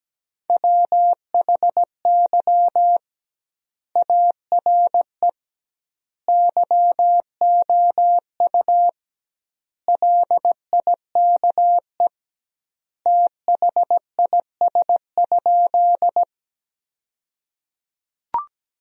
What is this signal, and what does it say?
Morse code, heard as a keyed tone, about 17 wpm, 700 Hz. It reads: WHY ARE YOU LIKE THIS?
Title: 2x Word-Spacing - Sentences from Top 200 Words - 17wpm